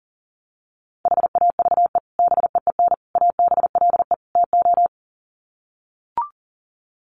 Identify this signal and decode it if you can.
Morse code, heard as a keyed tone, about 40 wpm, 700 Hz. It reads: HAVE BEEN ABLE TO
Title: Sets of 4 Words 40wpm